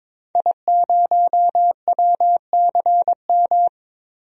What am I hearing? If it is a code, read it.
Morse code, heard as a keyed tone, about 21 wpm, 700 Hz. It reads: I0WCM
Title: DXCC-100 Call Signs - 5 Characters - 22wpm